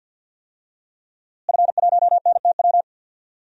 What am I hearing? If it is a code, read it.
Morse code, heard as a keyed tone, about 52 wpm, 700 Hz. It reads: F1NTW
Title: DXCC-100 Call Signs - 5 Characters - Repeated 2x - 50wpm